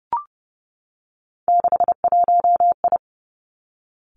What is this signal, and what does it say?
Morse code, heard as a keyed tone, about 30 wpm, 700 Hz. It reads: 61I
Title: Three Challenging Letter-Number Combinations 30wpm